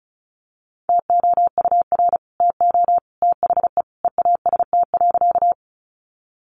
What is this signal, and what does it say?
Morse code, heard as a keyed tone, about 34 wpm, 700 Hz. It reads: TOUR TO THE EAST.
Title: Sentences from Top 1400 Words 35wpm